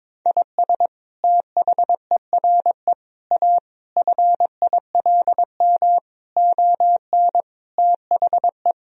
Morse code, heard as a keyed tone, about 22 wpm, 700 Hz. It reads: IS THERE A FILM ON THE
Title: Sentences from Top 1000 Words - Review Part 3 - 22wpm